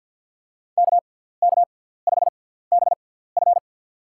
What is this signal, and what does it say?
Morse code, heard as a keyed tone, about 53 wpm, spaced as for 17 wpm, 700 Hz. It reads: KK5BF
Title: USA Call Signs - 6x Character Spacing - 50wpm